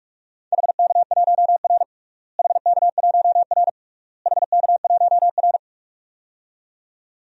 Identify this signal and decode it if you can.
Morse code, heard as a keyed tone, about 42 wpm, 700 Hz. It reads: HK1R HK1R HK1R
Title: DXCC-100 Call Signs - 4 Characters - Repeated 3x - 45wpm